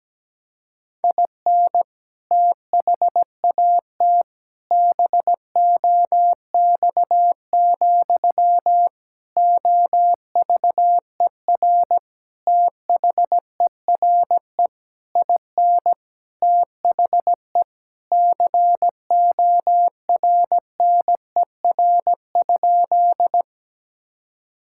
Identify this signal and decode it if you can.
Morse code, heard as a keyed tone, about 17 wpm, 700 Hz. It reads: IN THAT BOX, OVER THERE IN THE CORNER?
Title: Sentences from Top 1600 Words 17wpm